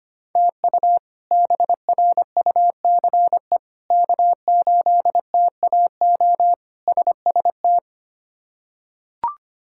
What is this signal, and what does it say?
Morse code, heard as a keyed tone, about 25 wpm, 700 Hz. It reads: TU BRUCE K8TAO SST
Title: K1USN SST QSO Practice - 25wpm